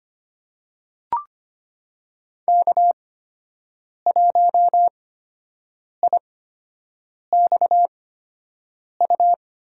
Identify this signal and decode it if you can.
Morse code, heard as a keyed tone, about 25 wpm, spaced as for 7 wpm, 700 Hz. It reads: K1IXU
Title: USA Call Signs - 5 Characters - 8x Character Spacing - 25wpm